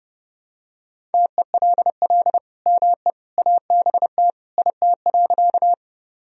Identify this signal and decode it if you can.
Morse code, heard as a keyed tone, about 32 wpm, 700 Hz. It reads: TELL ME ABT IT.
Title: Sentences from Top 200 Words and Abbreviations 30wpm